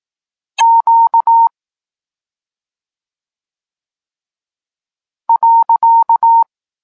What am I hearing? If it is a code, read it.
Morse code, heard as a keyed tone, about 18 wpm, 920 Hz. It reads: Q .
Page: Tekst 32 – Doskonalenie odbioru poznanych liter